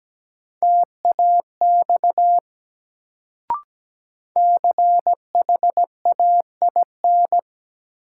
Morse code, heard as a keyed tone, about 17 wpm, 700 Hz. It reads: TAX CHAIN
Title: Sentences from Top 1500 Words - New Word Drill Alt - 17wpm